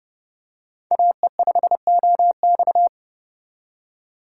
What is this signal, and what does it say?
Morse code, heard as a keyed tone, about 29 wpm, 700 Hz. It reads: AE5OX
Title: USA Call Signs - 3x Character Spacing - 30wpm